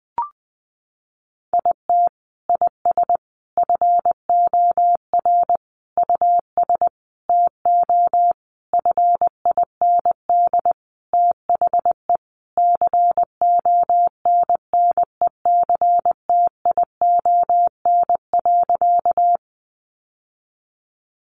Morse code, heard as a keyed tone, about 20 wpm, 700 Hz. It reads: IT IS FOR US TO FIND THE CONNECTION.
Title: Sentences from Top 1300 Words 20wpm